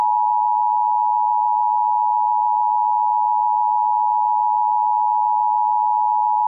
Emergency Alert System (SAME) tones (the 853 and 960 Hz attention tone). It is heard from 1.1 to 5.8 s.